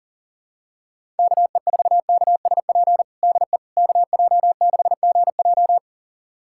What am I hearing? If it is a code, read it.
Morse code, heard as a keyed tone, about 40 wpm, 700 Hz. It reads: KE4KSP DE KJ6GJ
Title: USA Callsign DE Callsign 40wpm